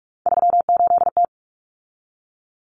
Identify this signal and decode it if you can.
Morse code, heard as a keyed tone, about 44 wpm, 700 Hz. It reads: 38T
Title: Common Cut Numbers - 3-Digit 45wpm